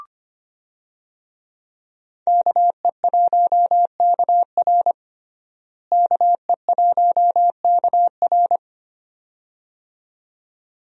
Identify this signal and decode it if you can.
Morse code, heard as a keyed tone, about 25 wpm, 700 Hz. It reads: KE1KR KE1KR
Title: USA Call Signs - 5 Characters - Repeated 2x - 25wpm